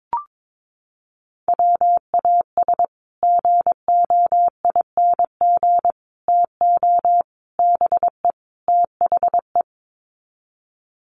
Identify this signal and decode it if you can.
Morse code, heard as a keyed tone, about 22 wpm, 700 Hz. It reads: WAS GOING TO BE THE
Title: Sets of 5 Words Encore2 22wpm